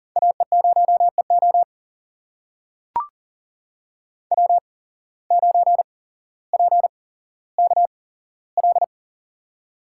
Morse code, heard as a keyed tone, about 38 wpm, spaced as for 17 wpm, 700 Hz. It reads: AE0EO W9PKL
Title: USA Call Signs - 5 Characters - 8x Character Spacing - 40wpm